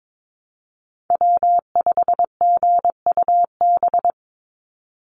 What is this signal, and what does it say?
Morse code, heard as a keyed tone, about 22 wpm, 700 Hz. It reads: W5GUB
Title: USA Call Signs 22wpm